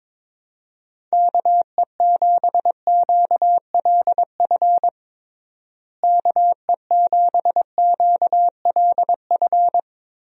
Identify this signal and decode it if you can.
Morse code, heard as a keyed tone, about 22 wpm, 700 Hz. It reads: KE7QLF KE7QLF
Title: USA Call Signs - 6 Characters - Repeated 2x - 22wpm